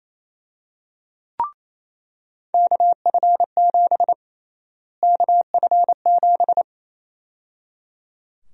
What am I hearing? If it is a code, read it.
Morse code, heard as a keyed tone, about 28 wpm, 700 Hz. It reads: KF7 KF7
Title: USA Call Sign Prefixes - Repeated 2x - 28wpm